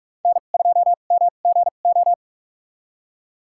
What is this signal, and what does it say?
Morse code, heard as a keyed tone, about 42 wpm, spaced as for 31 wpm, 700 Hz. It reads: N2MGO
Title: USA Call Signs - 5 Characters - 2x Character Spacing - 45wpm